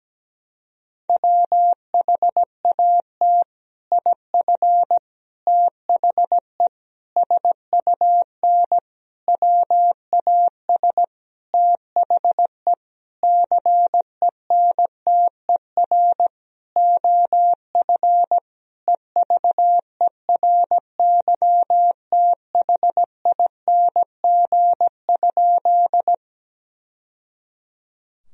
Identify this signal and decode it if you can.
Morse code, heard as a keyed tone, about 17 wpm, 700 Hz. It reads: WHAT IF THE SUN WAS THE CENTER OF EVERYTHING?
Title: Sentences from Top 1000 Words - Review Part 2 - 17wpm